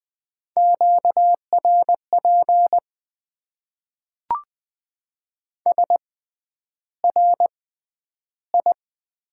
Morse code, heard as a keyed tone, about 20 wpm, spaced as for 10 wpm, 700 Hz. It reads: QRP SRI
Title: Top 90 QSO Elements - 6x Character Spacing - 20wpm